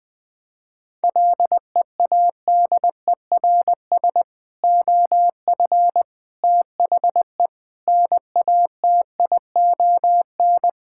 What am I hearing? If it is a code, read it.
Morse code, heard as a keyed tone, about 20 wpm, 700 Hz. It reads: LEADERS OF THE NATION
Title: Sentences with 3-Word Phrases - 20wpm